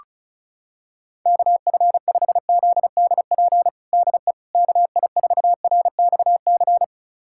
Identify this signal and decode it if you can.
Morse code, heard as a keyed tone, about 35 wpm, 700 Hz. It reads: KF5ZDP DE KI4RXC
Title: USA Callsign DE Callsign 35wpm